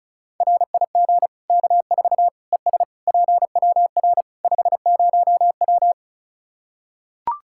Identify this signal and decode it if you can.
Morse code, heard as a keyed tone, about 36 wpm, 700 Hz. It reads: RIG K4 ES PWR 50W
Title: K1USN SST QSO Practice 35wpm